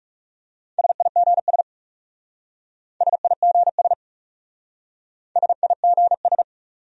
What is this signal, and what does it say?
Morse code, heard as a keyed tone, about 40 wpm, 700 Hz. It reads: SIGS SIGS SIGS